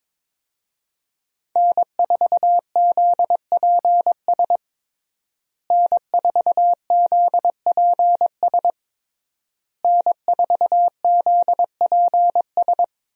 Morse code, heard as a keyed tone, about 21 wpm, 700 Hz. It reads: N4ZPS N4ZPS N4ZPS
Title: USA Call Signs - 5 Characters - Repeated 3x - 22wpm